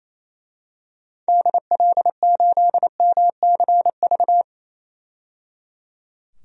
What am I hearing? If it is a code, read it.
Morse code, heard as a keyed tone, about 27 wpm, 700 Hz. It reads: DL8MCV
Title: DXCC-100 Call Signs - 6 Characters - 28wpm